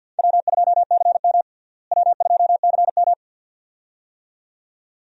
Morse code, heard as a keyed tone, about 51 wpm, 700 Hz. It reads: W2CM W2CM
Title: USA Call Signs - 4 Characters - Repeated 2x - 50wpm